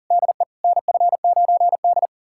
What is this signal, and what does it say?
Morse code, heard as a keyed tone, about 40 wpm, 700 Hz. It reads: DE NF9D
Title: USA Callsign DE Callsign - 4 Characters - 40wpm